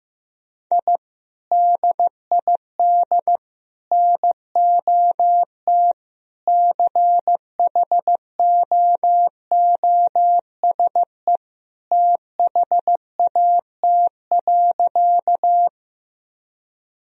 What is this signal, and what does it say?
Morse code, heard as a keyed tone, about 15 wpm, 700 Hz. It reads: I DID NOT CHOOSE THAT.
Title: Sentences from Top 1000 Words - Review Part 4 - 15wpm